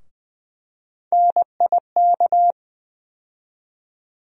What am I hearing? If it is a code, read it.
Morse code, heard as a keyed tone, about 20 wpm, 700 Hz. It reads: NIK
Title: USA Call Sign Postfixes - 3 Letters 20wpm